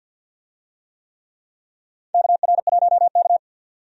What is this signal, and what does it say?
Morse code, heard as a keyed tone, about 52 wpm, 700 Hz. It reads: KR1K
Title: USA Call Signs - 4 Characters - 50wpm